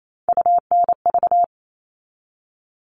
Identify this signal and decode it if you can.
Morse code, heard as a keyed tone, about 28 wpm, 700 Hz. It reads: UNV
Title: All Cut Numbers - 3-Digit 28wpm